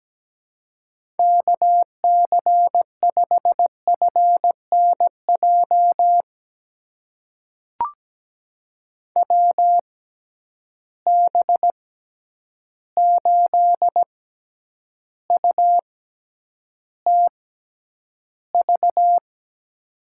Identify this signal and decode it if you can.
Morse code, heard as a keyed tone, about 17 wpm, spaced as for 9 wpm, 700 Hz. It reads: KC5FNJ WB8UTV
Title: USA Call Signs - 6 Characters - 6x Character Spacing - 17wpm